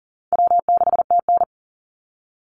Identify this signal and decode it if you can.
Morse code, heard as a keyed tone, about 40 wpm, 700 Hz. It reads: W6TN
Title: Easy Call Signs - 40wpm